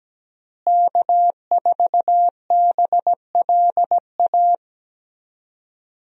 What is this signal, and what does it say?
Morse code, heard as a keyed tone, about 17 wpm, 700 Hz. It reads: K4BLA
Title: USA Call Signs - 5 Characters - Repeated 3x - 17wpm